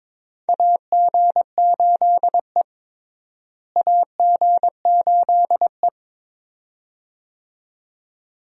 Morse code, heard as a keyed tone, about 22 wpm, 700 Hz. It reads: AG8E AG8E
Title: USA Call Signs - 4 Characters - Repeated 2x - 22wpm